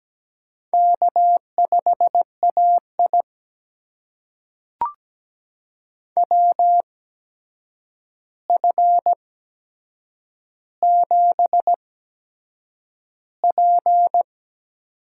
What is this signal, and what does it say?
Morse code, heard as a keyed tone, about 17 wpm, spaced as for 7 wpm, 700 Hz. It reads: K5AI WF7P
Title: USA Call Signs - 4 Characters - 8x Character Spacing - 17wpm